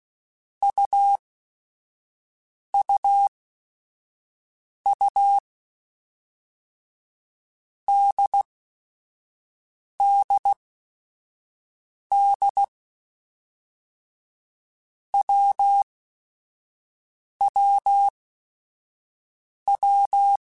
Morse code, heard as a keyed tone, about 16 wpm, spaced as for 5 wpm, 800 Hz. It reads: UUU DDD WWW